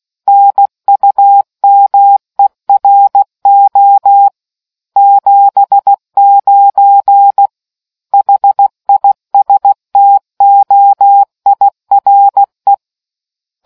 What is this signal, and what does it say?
Morse code, heard as a keyed tone, about 16 wpm, 800 Hz. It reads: NUMERO 79 HISTOIRE